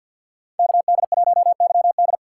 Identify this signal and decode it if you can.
Morse code, heard as a keyed tone, about 47 wpm, 700 Hz. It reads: KD1YD
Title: USA Call Signs - 5 Characters - 3x Character Spacing - 50wpm